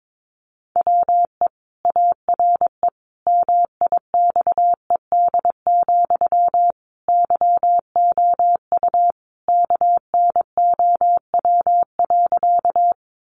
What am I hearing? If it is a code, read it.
Morse code, heard as a keyed tone, about 22 wpm, 700 Hz. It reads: WE ARE MIXED, YOU KNOW.
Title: Sentences from Top 1500 Words 22wpm